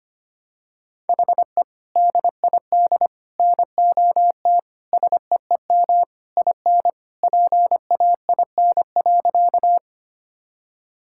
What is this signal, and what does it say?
Morse code, heard as a keyed tone, about 25 wpm, 700 Hz. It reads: HE DID NOT SEEM IN PAIN.
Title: Sentences from Top 1600 Words 25wpm